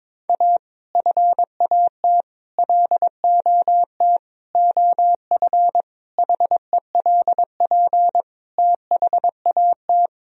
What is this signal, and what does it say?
Morse code, heard as a keyed tone, about 22 wpm, 700 Hz. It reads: A FAT LOT OF HELP THAT
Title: Sentences from Top 1000 Words - Review Part 4 - 22wpm